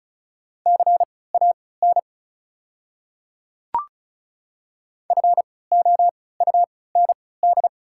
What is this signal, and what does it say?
Morse code, heard as a keyed tone, about 36 wpm, spaced as for 20 wpm, 690 Hz. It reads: CAN FOUND
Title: 3x Character Spacing - Top 300 Words - 35wpm